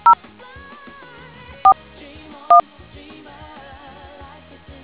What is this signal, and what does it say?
DTMF, heard as telephone touch tones, tones of about 100 ms. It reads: 0 4 1